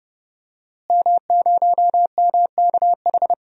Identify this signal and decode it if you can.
Morse code, heard as a keyed tone, about 29 wpm, 700 Hz. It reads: M0MKH
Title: DXCC-100 Call Signs - 5 Characters - Repeated 2x - 30wpm